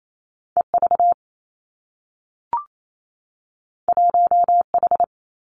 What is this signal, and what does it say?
Morse code, heard as a keyed tone, about 28 wpm, 700 Hz. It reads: E V 1 H
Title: Two Challenging Letter-Number Combinations 28wpm